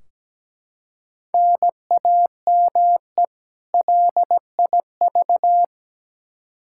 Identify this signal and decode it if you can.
Morse code, heard as a keyed tone, about 17 wpm, 700 Hz. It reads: NAME LIV
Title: Practice QSO - Exchange 1 - Beginning - 17wpm